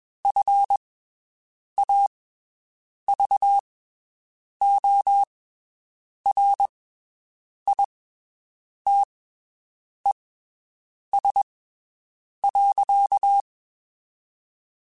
Morse code, heard as a keyed tone, about 21 wpm, spaced as for 7 wpm, 800 Hz. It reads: FAVORITES.
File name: (spaced) / QSOs Spaced-mp3s / WD0FBT 1 @20 sp